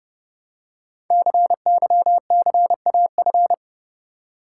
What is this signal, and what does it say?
Morse code, heard as a keyed tone, about 28 wpm, 700 Hz. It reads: CYCAF
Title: Code Groups - 5 Letters - 30wpm